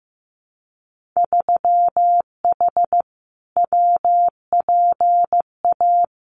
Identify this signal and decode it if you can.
Morse code, heard as a keyed tone, about 15 wpm, 700 Hz. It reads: 3H WPA
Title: Winter Field Day Exchanges 15wpm